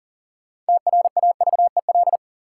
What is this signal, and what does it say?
Morse code, heard as a keyed tone, about 40 wpm, 700 Hz. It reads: TRAVEL
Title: Sentences from Top 400 Words - New Word Drill - 40wpm